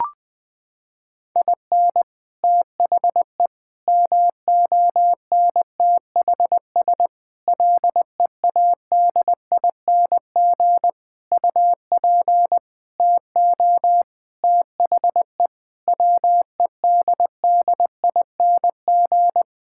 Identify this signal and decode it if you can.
Morse code, heard as a keyed tone, about 20 wpm, 700 Hz. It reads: IN THE MONTHS LEADING UP TO THE WEDDING
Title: Sentences with 6-Word Phrases - 20wpm